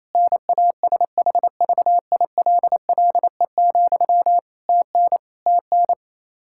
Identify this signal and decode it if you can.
Morse code, heard as a keyed tone, about 27 wpm, 700 Hz. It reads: NASHVILLE, TN TN
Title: Practice QSO - Exchange 1 - Beginning - 28wpm